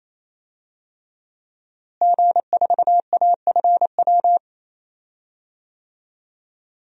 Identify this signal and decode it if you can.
Morse code, heard as a keyed tone, about 28 wpm, 700 Hz. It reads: G4AFW
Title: DXCC-100 Call Signs - 5 Characters - Repeated 3x - 28wpm